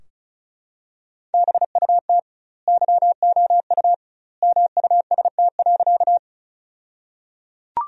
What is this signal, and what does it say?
Morse code, heard as a keyed tone, about 34 wpm, 700 Hz. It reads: BUT YOU MUST.
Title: 2x Word-Spacing - Sentences from Top 200 Words - 35wpm